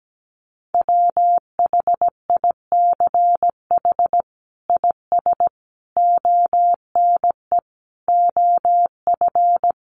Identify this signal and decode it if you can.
Morse code, heard as a keyed tone, about 17 wpm, 700 Hz. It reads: WHICH IS ONE OF
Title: Sets of 5 Words Encore 17wpm